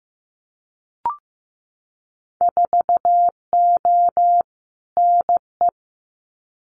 Morse code, heard as a keyed tone, about 15 wpm, 700 Hz. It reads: 4O NE
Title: Winter Field Day Exchanges Encore 15wpm